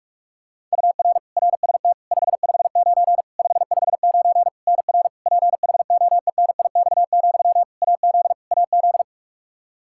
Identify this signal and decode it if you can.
Morse code, heard as a keyed tone, about 46 wpm, 700 Hz. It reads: UR RST 559 559 NR PHOENIX, AZ AZ
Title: Practice QSO - Exchange 1 - Beginning - 45wpm